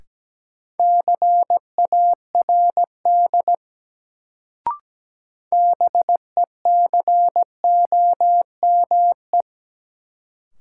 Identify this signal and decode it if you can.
Morse code, heard as a keyed tone, about 17 wpm, 700 Hz. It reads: CARD BECOME
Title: Sentences from Top 600 Words - New Word Drill - 17wpm